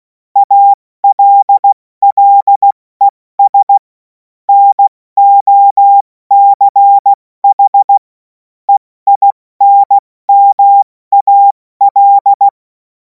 Morse code, heard as a keyed tone, about 16 wpm, spaced as for 14 wpm, 800 Hz. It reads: ALLES NOCH EINMAL
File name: Morsekurs / Witze / Witz 10. März 2026 70 F80